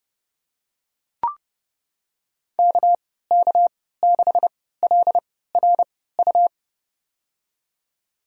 Morse code, heard as a keyed tone, about 32 wpm, spaced as for 17 wpm, 700 Hz. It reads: KK6LRU
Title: USA Call Signs - 6 Characters - 3x Character Spacing - 30wpm